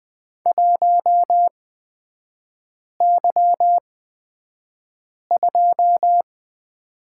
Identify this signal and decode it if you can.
Morse code, heard as a keyed tone, about 20 wpm, spaced as for 5 wpm, 700 Hz. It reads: 1Y2
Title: 243 Lesson 39e - Letter z - Three Characters 20wpm